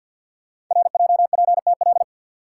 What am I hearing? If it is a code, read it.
Morse code, heard as a keyed tone, about 48 wpm, 700 Hz. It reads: RJPTL
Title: Code Groups - 5 Letters - 50wpm